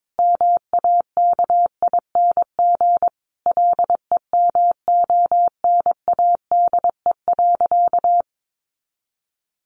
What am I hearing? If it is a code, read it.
Morse code, heard as a keyed tone, about 22 wpm, 700 Hz. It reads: MAKING LEMONADE.
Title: Additional Easy Phrases 22wpm